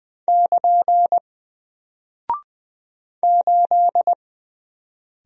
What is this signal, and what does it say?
Morse code, heard as a keyed tone, about 20 wpm, 700 Hz. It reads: <KN> 8
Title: Single Character (Letter:Number:Punctuation:Prosign) Alt 20wpm